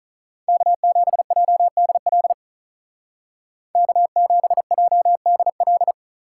Morse code, heard as a keyed tone, about 45 wpm, 700 Hz. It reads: K7JDL K7JDL